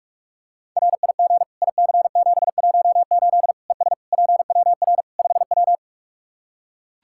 Morse code, heard as a keyed tone, about 41 wpm, 700 Hz. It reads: RIG IC718 ES PWR 5W
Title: K1USN SST QSO Practice 45wpm